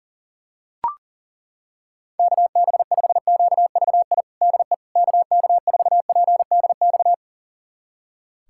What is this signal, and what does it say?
Morse code, heard as a keyed tone, about 37 wpm, 700 Hz. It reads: KB5QVI DE KK4PDX
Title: USA Callsign DE Callsign - 6 Characters - 40wpm